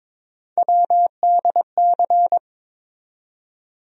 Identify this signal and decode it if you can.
Morse code, heard as a keyed tone, about 22 wpm, 700 Hz. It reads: WDC
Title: USA Call Sign Postfixes - 3 Letters 22wpm